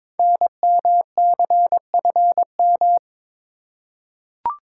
Morse code, heard as a keyed tone, about 22 wpm, 700 Hz. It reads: NMCFM
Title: Code Groups - 5 Letters - 22wpm